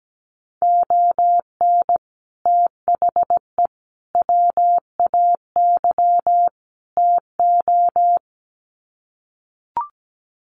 Morse code, heard as a keyed tone, about 17 wpm, 700 Hz. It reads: ON THE WAY TO
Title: Sets of 4 Words 17wpm